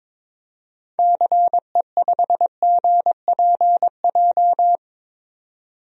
Morse code, heard as a keyed tone, about 21 wpm, 700 Hz. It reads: CE5GPJ
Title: DXCC-100 Call Signs - 6 Characters - 22wpm